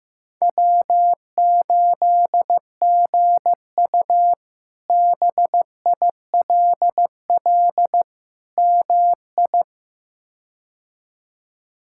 Morse code, heard as a keyed tone, about 16 wpm, 700 Hz. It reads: W8GU BILL MI
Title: K1USN SST QSO Practice - 15wpm